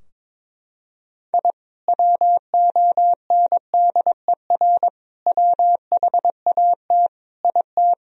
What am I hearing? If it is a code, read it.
Morse code, heard as a keyed tone, about 22 wpm, 700 Hz. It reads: I WONDER WHAT IT
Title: Sentences from Top 1000 Words - Review Part 3 - 22wpm